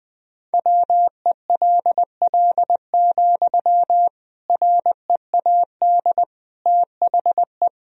Morse code, heard as a keyed tone, about 20 wpm, 700 Hz. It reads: WELL, READ THE
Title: Sentences from Top 300 Words and Abbreviations 20wpm